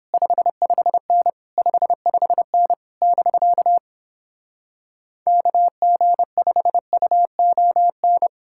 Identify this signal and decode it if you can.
Morse code, heard as a keyed tone, about 29 wpm, 700 Hz. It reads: 55N 55N <BK> KG5UON